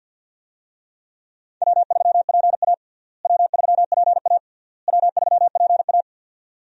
Morse code, heard as a keyed tone, about 49 wpm, 700 Hz. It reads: W3PA W3PA W3PA
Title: USA Call Signs - 4 Characters - Repeated 3x - 50wpm